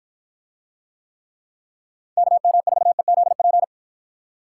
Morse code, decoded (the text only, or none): KM4EZP